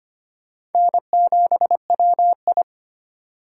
Morse code, heard as a keyed tone, about 31 wpm, 700 Hz. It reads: N7WI
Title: Speed Racing - USA 1x2 Call Signs 25wpm